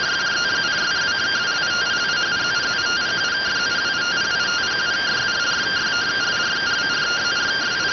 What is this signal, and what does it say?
Two-tone teleprinter signal, about 170 Hz shift.